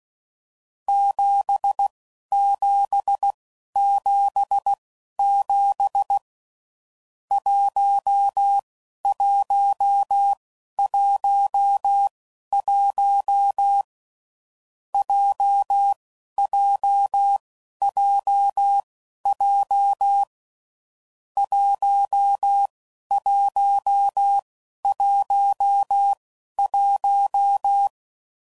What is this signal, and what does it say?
Morse code, heard as a keyed tone, about 16 wpm, spaced as for 11 wpm, 800 Hz. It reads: 7777 1111 JJJJ 1111
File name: Groups / Basic Letter Groups / JZ17